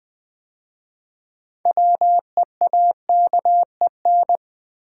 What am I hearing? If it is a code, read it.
Morse code, heard as a keyed tone, about 20 wpm, 700 Hz. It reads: WEAKEN
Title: 6 Letter Words - First 3 Letters Given - 20wpm